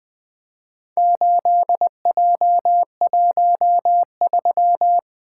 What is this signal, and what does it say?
Morse code, heard as a keyed tone, about 20 wpm, 700 Hz. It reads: 8J13
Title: Four Letter-Number Combinations - 20wpm